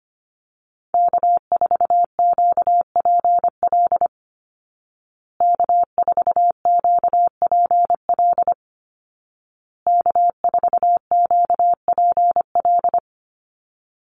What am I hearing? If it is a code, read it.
Morse code, heard as a keyed tone, about 25 wpm, 700 Hz. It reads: K4QPL K4QPL K4QPL
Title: Call Signs Repeated 2x - 25wpm